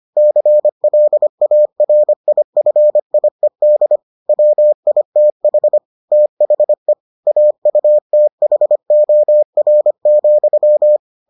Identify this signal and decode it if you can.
Morse code, heard as a keyed tone, about 25 wpm, 600 Hz. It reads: CLARIFIED WITH THE AUTHOR,